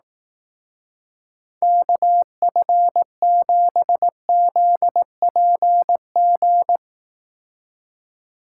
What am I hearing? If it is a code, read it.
Morse code, heard as a keyed tone, about 18 wpm, 700 Hz. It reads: KF7ZPG
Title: Speed Racing - USA 2x3 Call Signs 15wpm